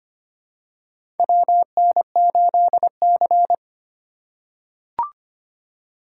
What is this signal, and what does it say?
Morse code, heard as a keyed tone, about 25 wpm, 700 Hz. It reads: WN8C
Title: USA Call Signs - 4 Characters - 2x Character Spacing - 25wpm